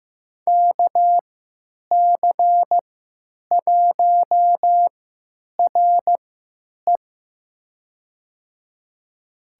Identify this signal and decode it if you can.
Morse code, heard as a keyed tone, about 15 wpm, spaced as for 8 wpm, 700 Hz. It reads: KC1RE
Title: USA Call Signs - 3x Character Spacing - 15wpm